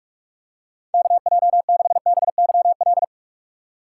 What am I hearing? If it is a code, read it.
Morse code, heard as a keyed tone, about 46 wpm, 700 Hz. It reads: KJ6BYL